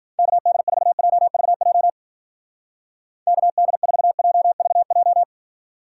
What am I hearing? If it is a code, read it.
Morse code, heard as a keyed tone, about 46 wpm, 700 Hz. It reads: KD4JVJ KD4JVJ
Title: Speed Racing - USA 2x3 Call Signs 40wpm